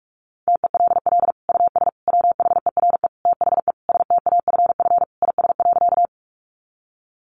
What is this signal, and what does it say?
Morse code, heard as a keyed tone, about 44 wpm, 700 Hz. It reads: TELL US WHERE THE STAFF IS.
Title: Sentences from Top 1200 Words 45wpm